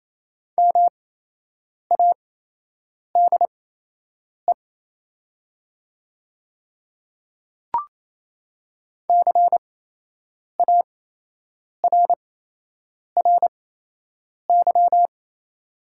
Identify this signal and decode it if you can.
Morse code, heard as a keyed tone, about 28 wpm, spaced as for 8 wpm, 700 Hz. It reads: MADE CARRY
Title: 8x Character Spacing - Top 300 Words - 28wpm